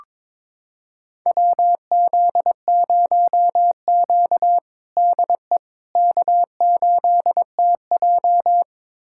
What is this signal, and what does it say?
Morse code, heard as a keyed tone, about 22 wpm, 700 Hz. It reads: WZ0Q DE K8TJ
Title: USA Callsign DE Callsign - 4 Characters - 22wpm